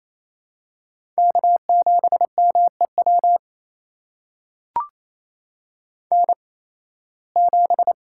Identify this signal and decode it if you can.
Morse code, heard as a keyed tone, about 28 wpm, spaced as for 18 wpm, 700 Hz. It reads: K7MEW N7
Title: USA Call Signs - 5 Characters - 8x Character Spacing - 28wpm